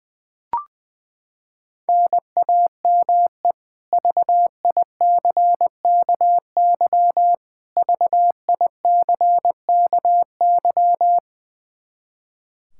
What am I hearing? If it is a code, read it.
Morse code, heard as a keyed tone, about 20 wpm, 700 Hz. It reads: NAME VICKY VICKY
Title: K1USN SST QSO Practice 20wpm